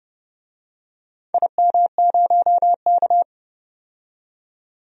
Morse code, heard as a keyed tone, about 32 wpm, 700 Hz. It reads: IM0K
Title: DXCC-100 Call Signs - 4 Characters - 30wpm